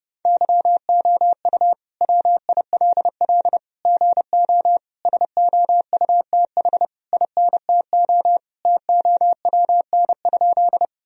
Morse code, heard as a keyed tone, about 30 wpm, 700 Hz. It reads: YOU WILL GO SOUTH INTO TOWN?
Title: Sentences from Top 1000 Words - Review Part 2 - 30wpm